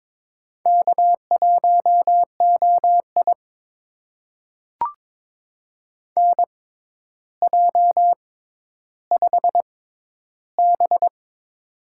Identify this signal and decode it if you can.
Morse code, heard as a keyed tone, about 22 wpm, spaced as for 11 wpm, 700 Hz. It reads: K1OI NJ5B
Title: USA Call Signs - 4 Characters - 6x Character Spacing - 22wpm